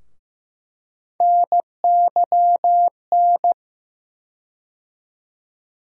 Morse code, heard as a keyed tone, about 15 wpm, 700 Hz. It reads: NYN
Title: USA Call Sign Postfixes - 3 Letters 15wpm